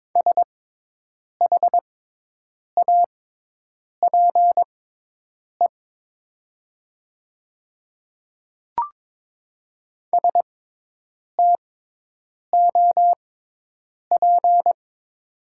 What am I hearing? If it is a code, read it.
Morse code, heard as a keyed tone, about 22 wpm, spaced as for 8 wpm, 700 Hz. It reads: SHAPE STOP
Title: 6x Character Spacing - Top 500 Words - 22wpm